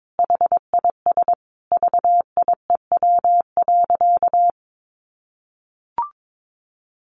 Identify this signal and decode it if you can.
Morse code, heard as a keyed tone, about 22 wpm, 700 Hz. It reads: HIS VIEW.
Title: Sentences from Top 1400 Words 22wpm